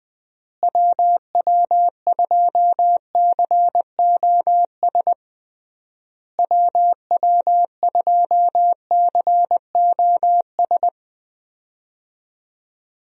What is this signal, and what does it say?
Morse code, heard as a keyed tone, about 20 wpm, 700 Hz. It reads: WW2COS WW2COS
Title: USA Call Signs - 6 Characters - Repeated 3x - 20wpm